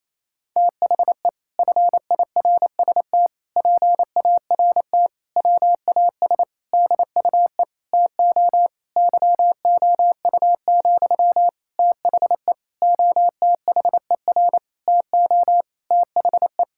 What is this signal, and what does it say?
Morse code, heard as a keyed tone, about 28 wpm, 700 Hz. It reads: THE FIRST PART WAS DUE TO YOU, THE OTHER TO THE
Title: Sentences from Top 1000 Words - Review Part 6 - 28wpm